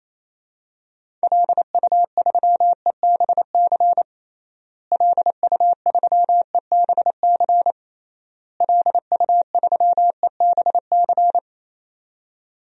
Morse code, heard as a keyed tone, about 28 wpm, 700 Hz. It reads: LU3EBC LU3EBC LU3EBC
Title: DXCC-100 Call Signs - 6 Characters - Repeated 3x - 28wpm